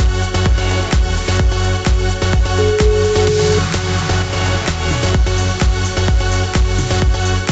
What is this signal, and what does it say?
Ringback tone (Europe, 425 Hz), with noise.